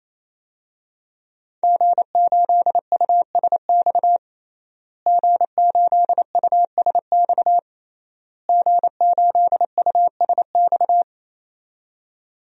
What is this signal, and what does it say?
Morse code, heard as a keyed tone, about 28 wpm, 700 Hz. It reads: G8USX G8USX G8USX
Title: DXCC-100 Call Signs - 5 Characters - Repeated 3x - 28wpm